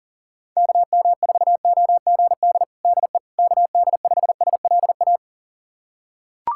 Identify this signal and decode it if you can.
Morse code, heard as a keyed tone, about 37 wpm, 700 Hz. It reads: KM4OGD DE KD5SLA
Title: USA Callsign DE Callsign - 6 Characters - 40wpm